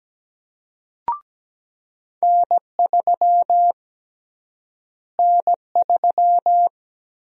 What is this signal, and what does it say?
Morse code, heard as a keyed tone, about 17 wpm, 700 Hz. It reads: N3 N3
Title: USA Call Sign Prefixes - Repeated 2x - 17wpm